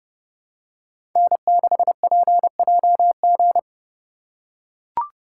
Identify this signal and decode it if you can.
Morse code, heard as a keyed tone, about 30 wpm, 700 Hz. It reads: N6PJG
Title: USA Call Signs - 5 Characters - 3x Character Spacing - 30wpm